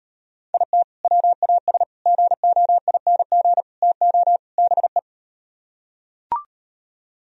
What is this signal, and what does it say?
Morse code, heard as a keyed tone, about 38 wpm, 700 Hz. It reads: IT WAS GOING TO BE